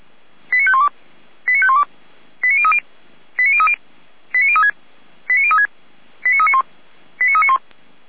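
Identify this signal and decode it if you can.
Five-tone selective call (ZVEI), heard as quick runs of five tones, tones of about 80 ms. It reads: 89512 89512 89030 89030 89036 89036 89391 89391
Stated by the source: ZVEI five-tone selective call received on an SDR.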